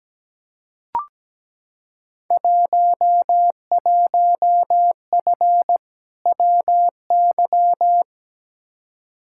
Morse code, heard as a keyed tone, about 17 wpm, 700 Hz. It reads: 11F WY
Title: Field Day Exchanges Encore - 17wpm